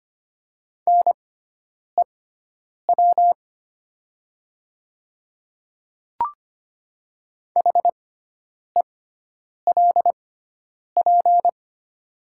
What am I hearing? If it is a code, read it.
Morse code, heard as a keyed tone, about 25 wpm, spaced as for 9 wpm, 700 Hz. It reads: NEW HELP
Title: 6x Character Spacing - Top 500 Words - 25wpm